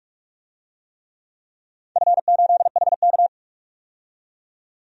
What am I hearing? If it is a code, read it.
Morse code, heard as a keyed tone, about 44 wpm, 700 Hz. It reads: F8HK
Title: DXCC-100 Call Signs - 4 Characters - 45wpm